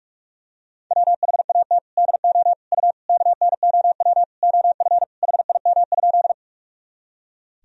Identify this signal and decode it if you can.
Morse code, heard as a keyed tone, about 42 wpm, 700 Hz. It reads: WHAT DO U KNOW OF HIM?
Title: Sentences from Top 100 Words and Abbreviations 45wpm